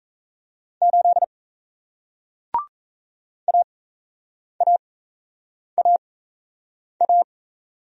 Morse code, heard as a keyed tone, about 37 wpm, spaced as for 8 wpm, 700 Hz. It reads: 8 AAAA